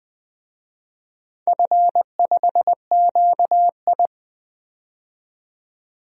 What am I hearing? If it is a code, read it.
Morse code, heard as a keyed tone, about 21 wpm, 700 Hz. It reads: F5QI
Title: DXCC-100 Call Signs - 4 Characters - 20wpm